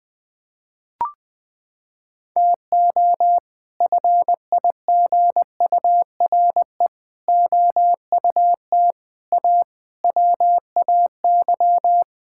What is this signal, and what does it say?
Morse code, heard as a keyed tone, about 20 wpm, 700 Hz. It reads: TO FIGURE OUT A WAY
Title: Sets of 6 Words - 20wpm